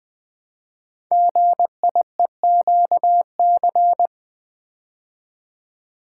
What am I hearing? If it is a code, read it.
Morse code, heard as a keyed tone, about 20 wpm, 700 Hz. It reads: GIEQC
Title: Code Groups - 5 Letter:Numbers - 20wpm